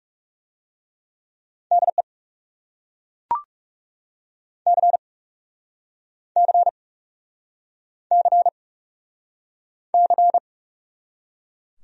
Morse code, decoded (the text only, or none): DE C C C C